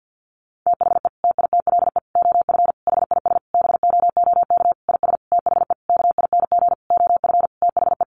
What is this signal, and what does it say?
Morse code, heard as a keyed tone, about 50 wpm, 700 Hz. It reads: THE TITLE OF HIS BOOK IS THE KING OF THE
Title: Sentences from Top 1200 Words 50wpm